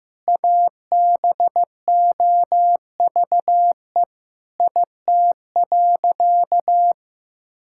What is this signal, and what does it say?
Morse code, heard as a keyed tone, about 15 wpm, 700 Hz. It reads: ABOVE IT.
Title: Sentences from Top 1600 Words 15wpm